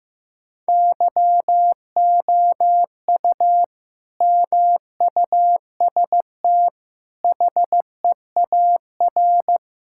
Morse code, heard as a keyed tone, about 15 wpm, 700 Hz. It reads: YOU MUST HEAR
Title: Sentences from Top 1600 Words 15wpm